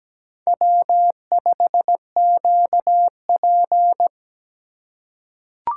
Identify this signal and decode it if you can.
Morse code, heard as a keyed tone, about 17 wpm, 700 Hz. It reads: W5QP
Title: K1USN SST QSO Practice - 17wpm